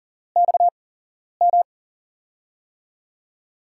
Morse code, heard as a keyed tone, about 40 wpm, spaced as for 11 wpm, 700 Hz. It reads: XM
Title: USA Call Signs - 5 Characters - 8x Character Spacing - 40wpm